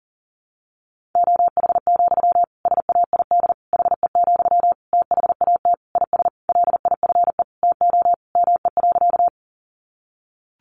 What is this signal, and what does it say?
Morse code, heard as a keyed tone, about 40 wpm, 700 Hz. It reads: OH, SAID HE, THAT IS LIFE TO ME.
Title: Sentences from Top 500 Words 40wpm